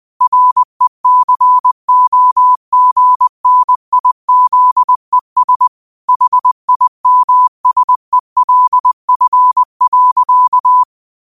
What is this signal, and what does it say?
Morse code, heard as a keyed tone, about 20 wpm, 1000 Hz. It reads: RECOGNIZES HIMSELF.